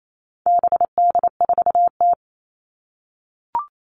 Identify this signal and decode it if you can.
Morse code, heard as a keyed tone, about 28 wpm, 700 Hz. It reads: BD4T
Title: All Cut Numbers - 4-Digit 28wpm